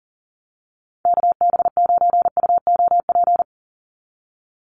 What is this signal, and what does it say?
Morse code, heard as a keyed tone, about 40 wpm, 700 Hz. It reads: KB9UOP
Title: USA Call Signs 40wpm